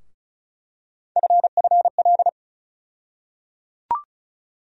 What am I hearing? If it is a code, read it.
Morse code, heard as a keyed tone, about 34 wpm, 700 Hz. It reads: FFL
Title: USA Call Sign Postfixes - 3 Letters 35wpm